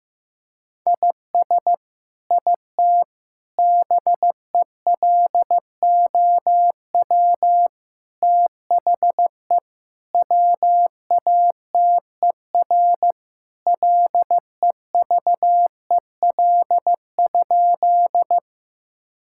Morse code, heard as a keyed tone, about 16 wpm, 700 Hz. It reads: IS IT BELOW THE WATER LEVEL?
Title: Sentences from Top 1000 Words - Review Part 5 - 15wpm